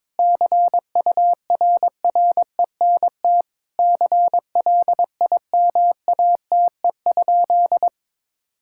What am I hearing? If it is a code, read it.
Morse code, heard as a keyed tone, about 22 wpm, 700 Hz. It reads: CURRENT CLIMATE?
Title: Sentences from Top 1600 Words 22wpm